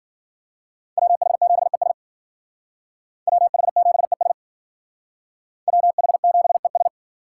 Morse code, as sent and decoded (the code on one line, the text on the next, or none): .-- .... --... ... / .-- .... --... ... / .-- .... --... ...
WH7ES WH7ES WH7ES